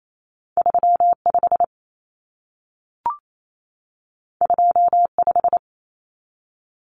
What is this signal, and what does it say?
Morse code, heard as a keyed tone, about 28 wpm, 700 Hz. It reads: 35 25
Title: Numbers - 2-Digit 28wpm